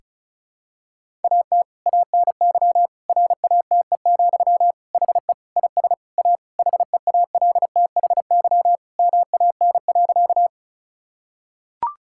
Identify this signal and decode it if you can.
Morse code, heard as a keyed tone, about 36 wpm, 700 Hz. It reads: AT ANY RATE, HE IS A HEALTHY MAN.
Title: Sentences from Top 1000 Words - Review Part 5 - 35wpm